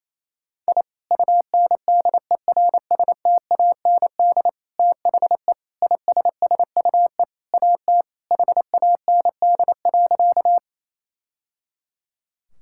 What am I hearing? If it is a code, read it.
Morse code, heard as a keyed tone, about 28 wpm, 700 Hz. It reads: I UNDERSTAND THE ISSUE AT HAND.
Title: Sentences from Top 1000 Words - Review Part 6 - 28wpm